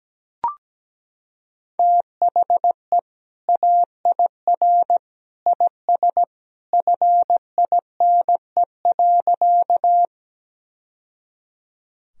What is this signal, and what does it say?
Morse code, heard as a keyed tone, about 17 wpm, 700 Hz. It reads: THE AIR IS FINE.
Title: Sentences from Top 1000 Words - Review Part 1 - 17wpm